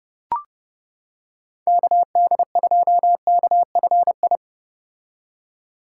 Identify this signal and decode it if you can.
Morse code, heard as a keyed tone, about 30 wpm, 700 Hz. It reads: KD2KFI